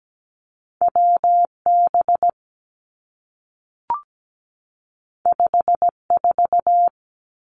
Morse code, heard as a keyed tone, about 17 wpm, 700 Hz. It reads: WB 54
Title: Two Challenging Letter-Number Combinations 17wpm